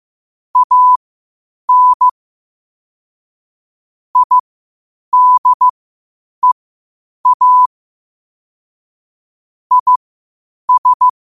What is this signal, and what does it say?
Morse code, heard as a keyed tone, about 15 wpm, spaced as for 8 wpm, 1000 Hz. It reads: AN IDEA IS
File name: qotdfiles / 8wpm